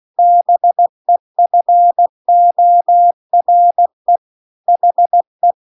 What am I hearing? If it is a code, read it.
Morse code, heard as a keyed tone, about 16 wpm, 700 Hz. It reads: BEFORE HE
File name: CW-The-Big-Twoheaded-River-2-16WPM